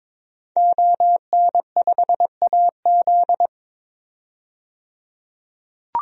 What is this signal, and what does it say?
Morse code, heard as a keyed tone, about 22 wpm, 700 Hz. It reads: ON5AZ
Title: DXCC-100 Call Signs - 5 Characters - 22wpm